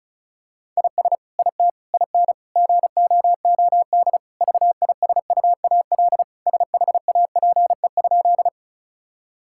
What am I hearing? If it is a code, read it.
Morse code, heard as a keyed tone, about 36 wpm, 700 Hz. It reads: IS IT IN GOOD VISUAL SHAPE?
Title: Sentences from Top 1000 Words - Review Part 6 - 35wpm